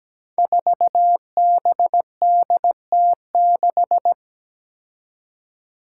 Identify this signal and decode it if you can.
Morse code, heard as a keyed tone, about 17 wpm, 700 Hz. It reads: 4BDT6
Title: Five Challenging Letter-Number Combinations 17wpm